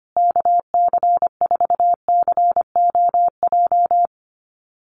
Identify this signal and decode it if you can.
Morse code, heard as a keyed tone, about 25 wpm, 700 Hz. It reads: KC4COJ
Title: USA Call Signs Repeated 2x 25wpm